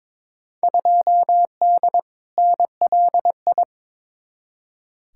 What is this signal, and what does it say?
Morse code, heard as a keyed tone, about 22 wpm, 700 Hz. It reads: 2D NLI
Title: Field Day Exchanges - 22wpm